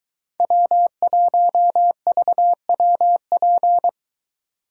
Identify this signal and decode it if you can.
Morse code, heard as a keyed tone, about 22 wpm, 700 Hz. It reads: W1VWP